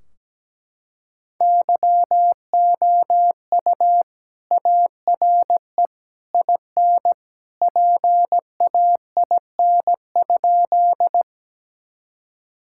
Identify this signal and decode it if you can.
Morse code, heard as a keyed tone, about 17 wpm, 700 Hz. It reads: YOU ARE IN PAIN?
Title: Sentences from Top 1600 Words 17wpm